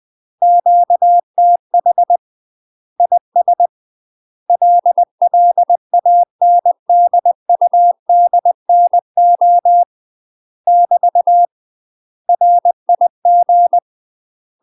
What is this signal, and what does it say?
Morse code, heard as a keyed tone, about 20 wpm, 700 Hz. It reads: QTH IS LLANDUDNO <BT> RIG
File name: UK MASTER QSOs TRAINING / 20WPM / UK MASTER QSOs 0045